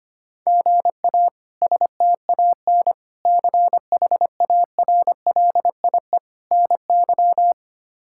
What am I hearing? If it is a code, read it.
Morse code, heard as a keyed tone, about 25 wpm, 700 Hz. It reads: GA STAN CHARLIE NY
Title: K1USN SST QSO Practice - 25wpm